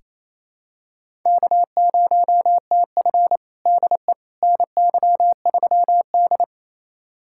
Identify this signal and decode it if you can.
Morse code, heard as a keyed tone, about 28 wpm, 700 Hz. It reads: K0TF DE NY3D
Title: USA Callsign DE Callsign - 4 Characters - 28wpm